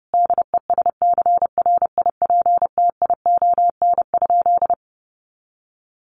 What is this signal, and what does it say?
Morse code, heard as a keyed tone, about 28 wpm, 700 Hz. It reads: DESCRIPTION?
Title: Sentences from Top 1300 Words 30wpm